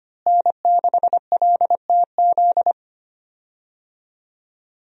Morse code, heard as a keyed tone, about 25 wpm, 700 Hz. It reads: N6LTZ
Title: USA Call Signs - 5 Characters - Repeated 3x - 25wpm